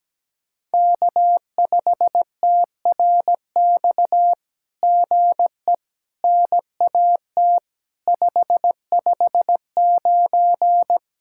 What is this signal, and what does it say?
Morse code, heard as a keyed tone, about 17 wpm, 700 Hz. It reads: K5TRX GE NAT 559
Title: SOTA QSO Practice 17wpm